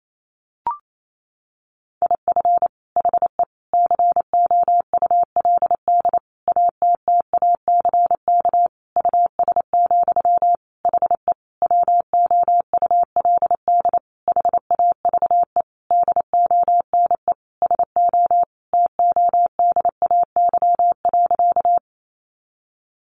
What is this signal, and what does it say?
Morse code, heard as a keyed tone, about 28 wpm, 700 Hz. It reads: IF HE COULD ATTACK US, HE WOULD HAVE DONE SO TODAY.
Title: Sentences from Top 1500 Words 28wpm